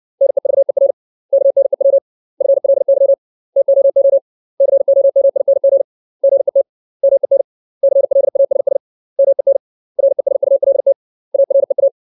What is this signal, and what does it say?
Morse code, heard as a keyed tone, about 60 wpm, 550 Hz. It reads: NEVER KNEW FLY TOO COMING GET MEN CLASS MEN RESULT AREA
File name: 632 words / 632 Words 60WPM